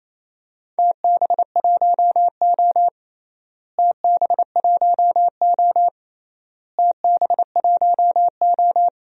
Morse code, heard as a keyed tone, about 28 wpm, 700 Hz. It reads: TB1O TB1O TB1O
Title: DXCC-100 Call Signs - 4 Characters - Repeated 3x - 28wpm